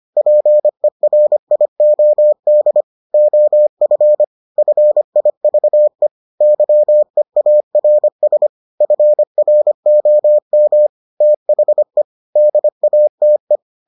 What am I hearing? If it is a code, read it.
Morse code, heard as a keyed tone, about 25 wpm, 600 Hz. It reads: PERIOD OF FIVE YEARS FROM THE DATE